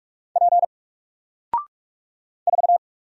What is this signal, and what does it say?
Morse code, heard as a keyed tone, about 45 wpm, 700 Hz. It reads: P 4
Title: Single Letter-Number Alt 45wpm